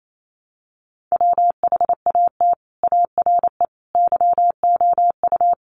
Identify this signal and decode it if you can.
Morse code, heard as a keyed tone, about 28 wpm, 700 Hz. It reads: WHAT ARE YOU
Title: Sets of 5 Words 28wpm